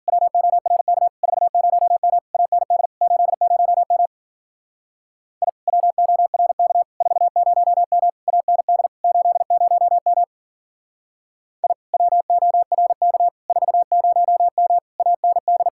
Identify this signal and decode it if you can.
Morse code, heard as a keyed tone, about 49 wpm, 700 Hz. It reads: WORK 40M AND 80M I WORK 40M AND 80M I WORK 40M AND
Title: Speed-Racing Common QSO Phrases Encore - 40wpm